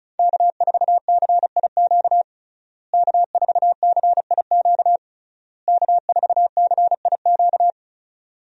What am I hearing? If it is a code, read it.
Morse code, heard as a keyed tone, about 34 wpm, 700 Hz. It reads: K4CIQ K4CIQ K4CIQ
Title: USA Call Signs - 5 Characters - Repeated 3x - 35wpm